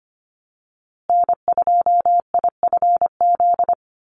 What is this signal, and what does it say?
Morse code, heard as a keyed tone, about 25 wpm, 700 Hz. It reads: N2IFZ
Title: USA Call Signs Repeated 3x 25wpm